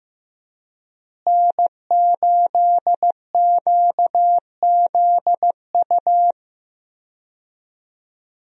Morse code, heard as a keyed tone, about 15 wpm, 700 Hz. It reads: N8QZU
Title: USA Call Signs - 5 Characters - Repeated 3x - 15wpm